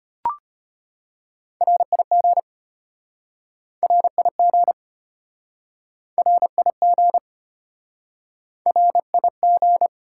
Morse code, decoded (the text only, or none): RIG RIG RIG RIG